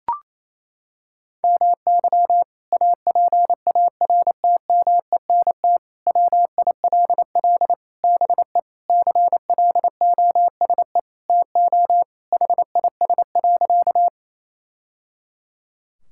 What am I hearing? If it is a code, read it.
Morse code, heard as a keyed tone, about 28 wpm, 700 Hz. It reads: MY APARTMENT WILL BE CLOSE TO HIS.
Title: Sentences from Top 1000 Words - Review Part 6 - 28wpm